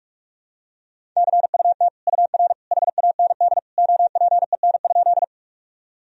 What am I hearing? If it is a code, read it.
Morse code, heard as a keyed tone, about 44 wpm, 700 Hz. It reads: CUT UR HAND OPEN?
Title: Sentences from Top 300 Words and Abbreviations 45wpm